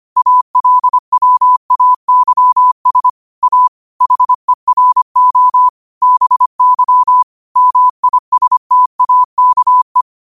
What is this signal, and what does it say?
Morse code, heard as a keyed tone, about 25 wpm, 1000 Hz. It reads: ALWAYS A HERO BY MISTAKE